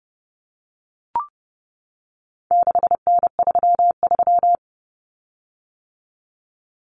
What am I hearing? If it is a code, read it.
Morse code, heard as a keyed tone, about 30 wpm, 700 Hz. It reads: 6N33
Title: Common Cut Numbers - 4-Digit 30wpm